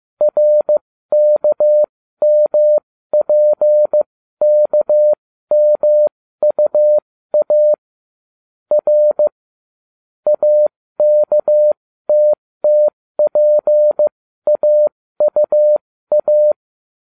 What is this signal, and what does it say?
Morse code, heard as a keyed tone, about 15 wpm, spaced as for 13 wpm, 600 Hz. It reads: RKMPKMUA R AKTTPAUA